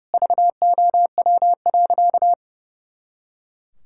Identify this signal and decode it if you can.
Morse code, heard as a keyed tone, about 30 wpm, 700 Hz. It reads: VOW.
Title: Pangram Challenge v2 - 30wpm